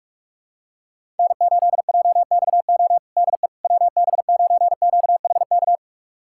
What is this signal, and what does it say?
Morse code, heard as a keyed tone, about 46 wpm, 700 Hz. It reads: N8JXO DE WB9QHK
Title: USA Callsign DE Callsign 45wpm